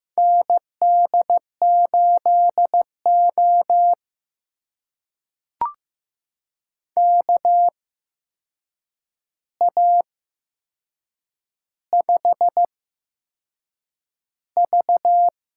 Morse code, heard as a keyed tone, about 15 wpm, spaced as for 6 wpm, 700 Hz. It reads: ND8O KA5V
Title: USA Call Signs - 4 Characters - 8x Character Spacing - 15wpm